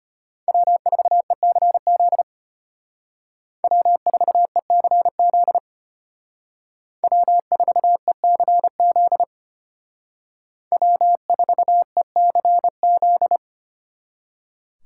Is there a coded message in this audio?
Yes: Morse code from 0.5 to 13.4 s, heard as a keyed tone, about 31 wpm, 700 Hz. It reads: W4ECZ W4ECZ W4ECZ W4ECZ